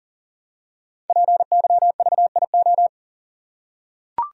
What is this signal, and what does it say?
Morse code, heard as a keyed tone, about 42 wpm, 690 Hz. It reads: PYVIO
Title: Code Groups - 5 Letter:Numbers - 40wpm